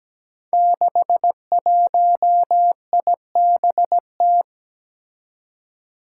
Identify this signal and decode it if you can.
Morse code, heard as a keyed tone, about 17 wpm, 700 Hz. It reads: 61IBT
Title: Five Challenging Letter-Number Combinations 17wpm